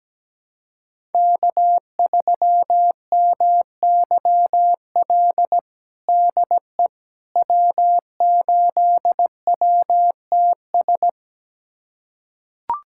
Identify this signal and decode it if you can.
Morse code, heard as a keyed tone, about 17 wpm, 700 Hz. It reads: K3MYL DE W8WTS
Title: USA Callsign DE Callsign - 5 Characters - 17wpm